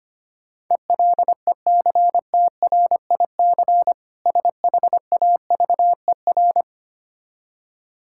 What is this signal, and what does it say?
Morse code, heard as a keyed tone, about 26 wpm, 700 Hz. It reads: ELECTRIC SHAVER
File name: practice / 25wpm / Difficult Words - Speed-Racing with Hints - 25wpm